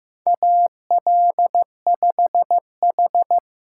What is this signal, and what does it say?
Morse code, heard as a keyed tone, about 15 wpm, 700 Hz. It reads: AL5H
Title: USA Call Signs - 6x Character Spacing - 15wpm